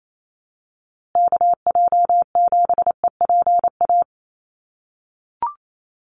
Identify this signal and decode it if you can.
Morse code, heard as a keyed tone, about 28 wpm, 700 Hz. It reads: KJ7EPA
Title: USA Call Signs Repeated 2x 28wpm